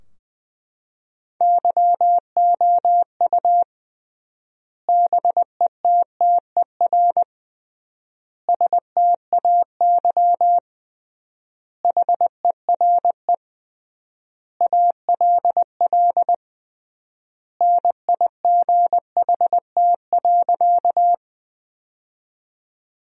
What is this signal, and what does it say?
Morse code, heard as a keyed tone, about 20 wpm, 700 Hz. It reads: YOU BETTER STAY HERE ALL NIGHT.
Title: 3x Word-Spacing - Sentences from Top 500 Words - 20wpm